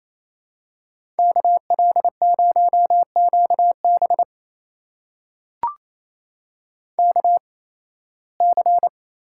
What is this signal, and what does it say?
Morse code, heard as a keyed tone, about 28 wpm, spaced as for 18 wpm, 700 Hz. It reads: KL0QB KC
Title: USA Call Signs - 8x Character Spacing - 28wpm